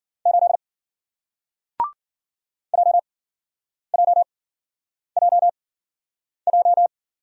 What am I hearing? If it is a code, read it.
Morse code, heard as a keyed tone, about 53 wpm, spaced as for 9 wpm, 700 Hz. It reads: 8 JJJJ